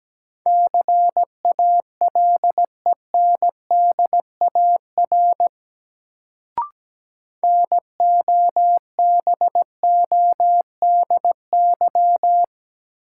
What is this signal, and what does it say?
Morse code, heard as a keyed tone, about 17 wpm, 700 Hz. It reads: CALENDAR NOBODY
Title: Sentences from Top 1600 Words - New Word Drill - 17wpm